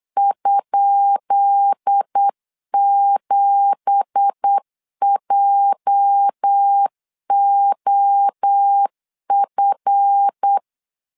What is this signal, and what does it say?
Morse code, heard as a keyed tone, about 8 wpm, 800 Hz. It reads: ?7JOF
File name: LEZIONI / 10 SIGNS / 10B INT 8wpm 3534